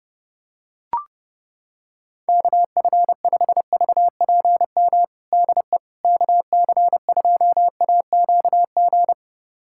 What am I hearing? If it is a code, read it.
Morse code, heard as a keyed tone, about 30 wpm, 690 Hz. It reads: KF5VPM DE KC2AQG
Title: USA Callsign DE Callsign - 6 Characters - 30wpm